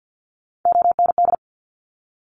Morse code, heard as a keyed tone, about 50 wpm, 700 Hz. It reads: OND
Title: Top Three Letter Combinations in Words 50wpm